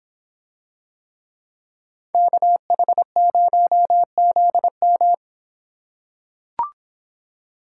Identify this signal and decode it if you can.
Morse code, heard as a keyed tone, about 24 wpm, 700 Hz. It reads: KH0ZM